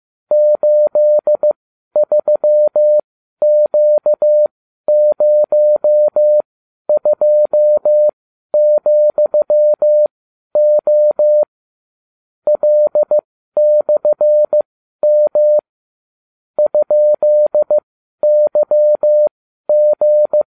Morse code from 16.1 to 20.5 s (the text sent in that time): ?YG